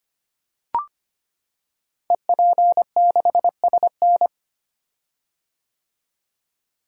Morse code, heard as a keyed tone, about 25 wpm, 700 Hz. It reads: EP6SN
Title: Code Groups - 5 Letter:Numbers - 25wpm